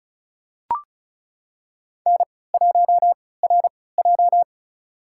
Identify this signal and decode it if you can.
Morse code, heard as a keyed tone, about 34 wpm, spaced as for 20 wpm, 700 Hz. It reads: N1RJ
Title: USA Call Signs - 4 Characters - 3x Character Spacing - 35wpm